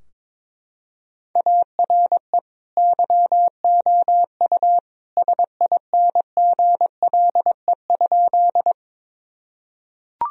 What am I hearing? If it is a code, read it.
Morse code, heard as a keyed tone, about 22 wpm, 700 Hz. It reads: ARE YOU SINGLE?
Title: Sentences from Top 1000 Words - Review Part 4 - 22wpm